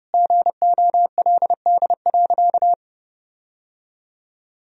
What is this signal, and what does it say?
Morse code, heard as a keyed tone, about 31 wpm, 700 Hz. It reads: GOLD.
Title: Sentences from Top 1000 Words - Review Part 3 - 30wpm